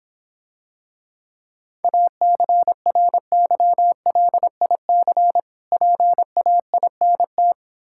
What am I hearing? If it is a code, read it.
Morse code, heard as a keyed tone, about 26 wpm, 700 Hz. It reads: ACRYLIC PAINT